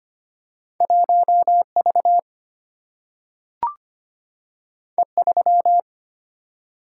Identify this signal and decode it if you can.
Morse code, heard as a keyed tone, about 25 wpm, 700 Hz. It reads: 1V E3
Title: Single-Letter Single-Number 25wpm